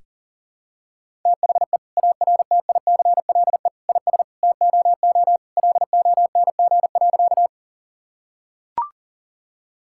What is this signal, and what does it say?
Morse code, heard as a keyed tone, about 40 wpm, 700 Hz. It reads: THE ARTICLE IS TOO LONG.
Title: Sentences from Top 1000 Words - Review Part 5 - 40wpm